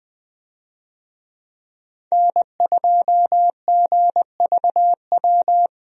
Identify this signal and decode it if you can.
Morse code, heard as a keyed tone, about 20 wpm, 700 Hz. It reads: N2GVW
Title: USA Call Signs - 5 Characters - Repeated 3x - 20wpm